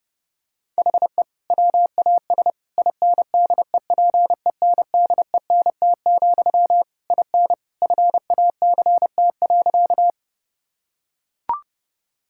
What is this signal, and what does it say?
Morse code, heard as a keyed tone, about 30 wpm, 700 Hz. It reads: HE WAS INDEPENDENT, IN FACT.
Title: Sentences from Top 1000 Words - Review Part 6 - 30wpm